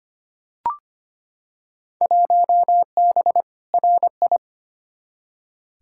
Morse code, heard as a keyed tone, about 25 wpm, 700 Hz. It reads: 1B RI
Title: Field Day Exchanges Encore - 25wpm